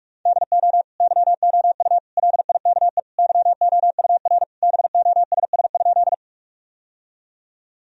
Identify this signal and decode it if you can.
Morse code, heard as a keyed tone, about 44 wpm, 700 Hz. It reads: DO YOU LIKE YOUR BOSS?
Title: Sentences from Top 1000 Words - Review Part 4 - 45wpm